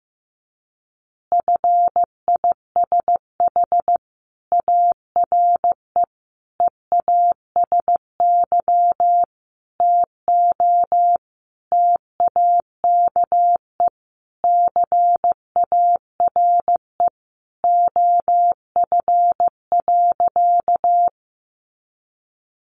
Morse code, heard as a keyed tone, about 14 wpm, 700 Hz. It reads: FISH ARE EASY TO TAKE CARE OF.
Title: Additional Easy Phrases 15wpm